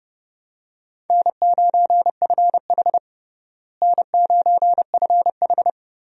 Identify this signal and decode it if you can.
Morse code, heard as a keyed tone, about 32 wpm, 700 Hz. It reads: N9FH N9FH
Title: USA Call Signs - 4 Characters - Repeated 3x - 30wpm